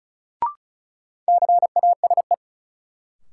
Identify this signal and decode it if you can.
Morse code, heard as a keyed tone, about 36 wpm, 700 Hz. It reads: CASE
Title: Sentences from Top 700 Words - New Word Drill - 35wpm